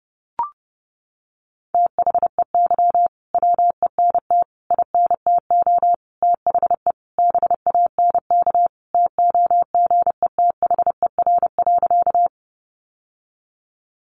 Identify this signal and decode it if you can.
Morse code, heard as a keyed tone, about 30 wpm, 700 Hz. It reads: THEY WENT INTO THE BANK TOGETHER.
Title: Sentences from Top 900 Words 30wpm